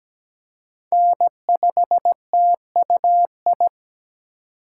Morse code, heard as a keyed tone, about 17 wpm, 700 Hz. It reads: N5TUI
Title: USA Call Signs - 2x Character Spacing - 17wpm